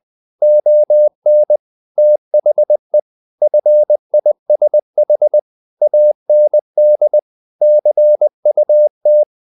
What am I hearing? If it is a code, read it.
Morse code, heard as a keyed tone, about 20 wpm, 600 Hz. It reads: ON THE FISH AND CUT
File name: the old man and the sea / the old man and the sea part 1 20wpm